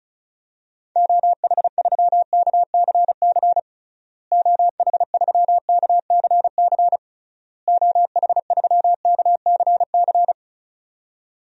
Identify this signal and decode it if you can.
Morse code, heard as a keyed tone, about 36 wpm, 700 Hz. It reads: OH3KCC OH3KCC OH3KCC
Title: DXCC-100 Call Signs - 6 Characters - Repeated 3x - 35wpm